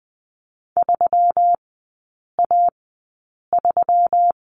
Morse code, heard as a keyed tone, about 20 wpm, spaced as for 8 wpm, 690 Hz. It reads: 3A3
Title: 133 Lesson 22e - Question Mark - Three Characters 20wpm